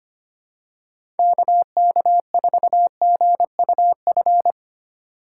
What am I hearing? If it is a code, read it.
Morse code, heard as a keyed tone, about 25 wpm, 700 Hz. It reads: KK4GUF
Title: USA Call Signs - 6 Characters - 3x Character Spacing - 25wpm